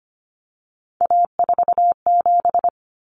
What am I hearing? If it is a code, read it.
Morse code, heard as a keyed tone, about 25 wpm, 700 Hz. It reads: A47
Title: Common Cut Numbers - 3-Digit 25wpm